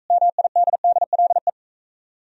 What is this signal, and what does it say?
Morse code, heard as a keyed tone, about 39 wpm, 690 Hz. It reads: MIDDLE